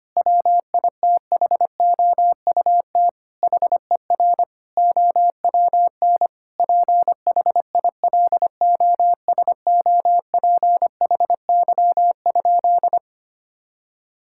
Morse code, heard as a keyed tone, about 25 wpm, 700 Hz. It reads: WITHOUT HER OWN PHILOSOPHY?
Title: Sentences from Top 1000 Words - Review Part 6 - 25wpm